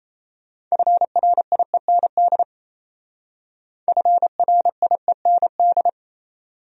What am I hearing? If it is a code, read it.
Morse code, heard as a keyed tone, about 35 wpm, 700 Hz. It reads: FRIEND FRIEND
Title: Speed-Racing Top 300 Words 28wpm